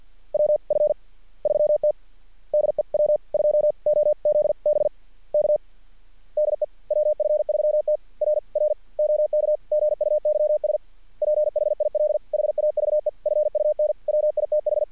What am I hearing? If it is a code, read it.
Morse code, heard as a keyed tone, about 46 wpm, 600 Hz. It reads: WF3T DE W2OZB K DE WF3T R R OK GUYS PHIL HAVE FUN WITH
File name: archived / CRSnet